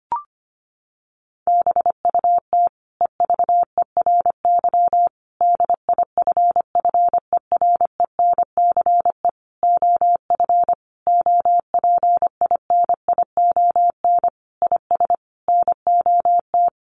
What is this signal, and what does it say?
Morse code, heard as a keyed tone, about 25 wpm, 700 Hz. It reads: BUT EVERY DIFFERENCE OF OPINION IS NOT
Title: Sentences from Top 1400 Words 25wpm